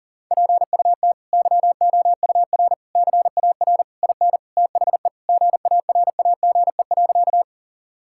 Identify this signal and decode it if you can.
Morse code, heard as a keyed tone, about 37 wpm, 700 Hz. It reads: PUT YOUR CAR IN THE GARAGE.
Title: Sentences from Top 1600 Words 40wpm